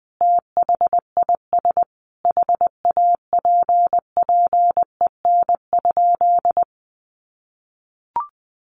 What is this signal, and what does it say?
Morse code, heard as a keyed tone, about 20 wpm, 700 Hz. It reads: THIS HAPPEN?
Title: Sentences from Top 400 Words 20wpm